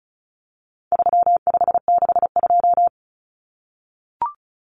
Morse code, heard as a keyed tone, about 34 wpm, 700 Hz. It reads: 3562
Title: Numbers - 4-Digit 35wpm